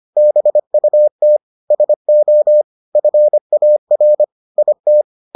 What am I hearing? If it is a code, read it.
Morse code, heard as a keyed tone, about 25 wpm, 600 Hz. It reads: BUT SO FAR IT